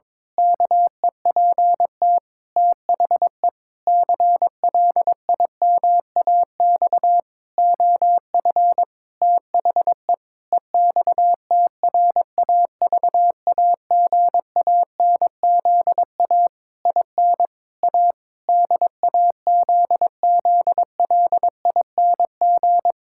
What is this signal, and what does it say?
Morse code, heard as a keyed tone, about 22 wpm, 700 Hz. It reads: KEPT THE CLIMAX OF THE EXTRAVAGANZA IN A DAZZLING
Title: Pangram Challenge v2 - 22wpm